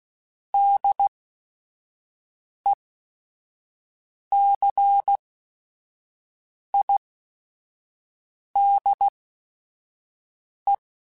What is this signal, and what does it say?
Morse code, heard as a keyed tone, about 16 wpm, spaced as for 5 wpm, 800 Hz. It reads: DECIDE